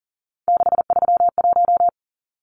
Morse code, heard as a keyed tone, about 40 wpm, 700 Hz. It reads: 631
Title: Numbers - 3-Digit 40wpm